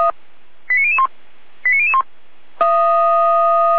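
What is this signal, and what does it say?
Five-tone selective call (ZVEI), heard as quick runs of five tones, tones of about 70 ms. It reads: 89001 89001
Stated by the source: ZVEI five-tone selective call received on an SDR.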